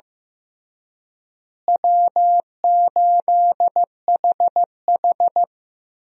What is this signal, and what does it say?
Morse code, heard as a keyed tone, about 16 wpm, 700 Hz. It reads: W8HH
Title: USA Call Signs - 4 Characters - Repeated 3x - 15wpm